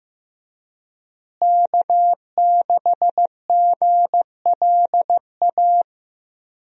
Morse code, heard as a keyed tone, about 15 wpm, 700 Hz. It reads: K6GLA
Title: USA Call Signs - 5 Characters - Repeated 3x - 15wpm